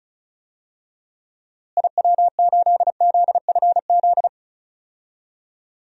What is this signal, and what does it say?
Morse code, heard as a keyed tone, about 36 wpm, 700 Hz. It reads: IW8ZFZ
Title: DXCC-100 Call Signs - 6 Characters - Repeated 3x - 35wpm